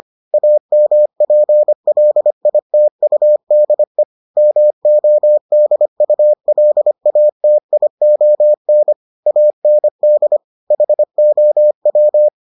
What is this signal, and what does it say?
Morse code, heard as a keyed tone, about 25 wpm, 600 Hz. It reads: AMPLITUDE MODULATION AND HOW